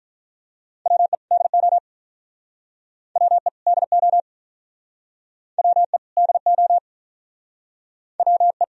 Morse code, heard as a keyed tone, about 50 wpm, 700 Hz. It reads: WE DO WE DO WE DO WE